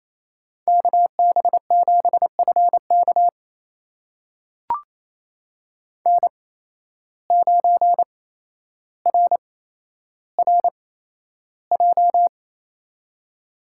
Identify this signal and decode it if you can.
Morse code, heard as a keyed tone, about 28 wpm, spaced as for 12 wpm, 700 Hz. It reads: KB7FK N9RRJ
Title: USA Call Signs - 5 Characters - 8x Character Spacing - 28wpm